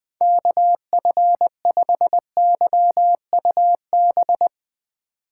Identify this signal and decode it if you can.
Morse code, heard as a keyed tone, about 20 wpm, 700 Hz. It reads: KF5YUB
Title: K1USN SST QSO Practice - 20wpm